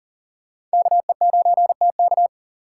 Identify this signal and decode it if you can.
Morse code, heard as a keyed tone, about 40 wpm, 700 Hz. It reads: KE9TK
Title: USA Call Signs - 5 Characters - 6x Character Spacing - 40wpm